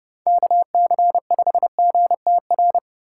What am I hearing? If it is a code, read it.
Morse code, heard as a keyed tone, about 35 wpm, 700 Hz. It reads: KC5GTR